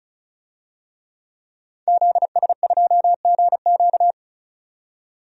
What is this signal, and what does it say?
Morse code, heard as a keyed tone, about 36 wpm, 700 Hz. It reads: ZS2GQ
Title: DXCC-100 Call Signs - 5 Characters - 35wpm